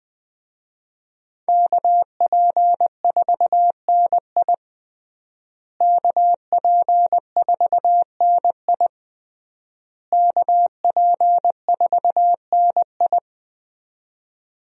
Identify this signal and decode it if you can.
Morse code, heard as a keyed tone, about 20 wpm, 700 Hz. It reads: KP4NI KP4NI KP4NI
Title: USA Call Signs - 5 Characters - Repeated 3x - 20wpm